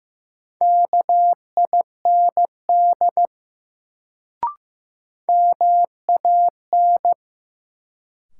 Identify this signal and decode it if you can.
Morse code, heard as a keyed tone, about 15 wpm, 700 Hz. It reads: KIND MAN
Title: Sets of 2 Words - New Word Drill - 15wpm